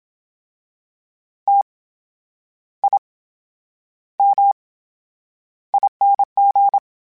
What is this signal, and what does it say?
Morse code, heard as a keyed tone, about 27 wpm, spaced as for 9 wpm, 800 Hz. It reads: TIMING
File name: Special INGs / Special INGs-mp3s / VerySlowToFaster Ing @25